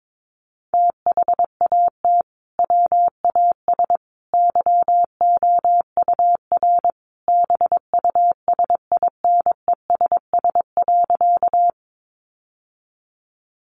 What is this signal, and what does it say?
Morse code, heard as a keyed tone, about 22 wpm, 700 Hz. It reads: THAT WAS YOUR BUSINESS.
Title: Sentences from Top 600 Words 22wpm